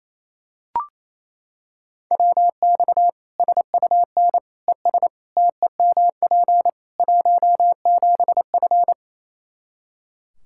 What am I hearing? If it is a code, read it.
Morse code, heard as a keyed tone, about 28 wpm, 700 Hz. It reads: WX SUN ES TEMP 17F
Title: K1USN SST QSO Practice 28wpm